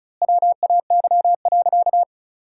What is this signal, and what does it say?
Morse code, heard as a keyed tone, about 36 wpm, 700 Hz. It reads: WAY.
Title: 3x Word-Spacing - Sentences from Top 100 Words - 35wpm